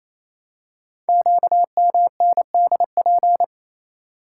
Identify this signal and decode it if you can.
Morse code, heard as a keyed tone, about 29 wpm, 700 Hz. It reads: QMNDP
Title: Code Groups - 5 Letters - 28wpm